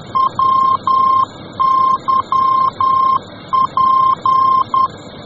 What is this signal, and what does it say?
Morse code, heard as a keyed tone, about 10 wpm, 1060 Hz. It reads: WYP